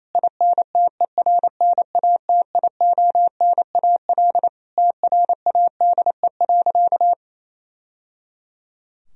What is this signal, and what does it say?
Morse code, heard as a keyed tone, about 28 wpm, 700 Hz. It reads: INTERNATIONAL TRADE.
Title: Sentences from Top 1000 Words - Review Part 4 - 28wpm